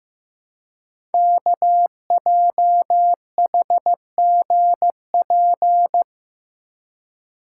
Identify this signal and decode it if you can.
Morse code, heard as a keyed tone, about 15 wpm, 700 Hz. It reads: KJHGP
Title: Code Groups - 5 Letters - 15wpm